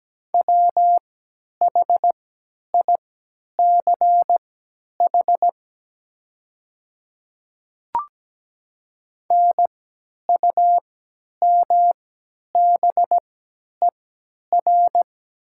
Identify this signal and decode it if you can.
Morse code, heard as a keyed tone, about 17 wpm, spaced as for 10 wpm, 700 Hz. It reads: WHICH NUMBER
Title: 3x Character Spacing - Top 100 Words - 17wpm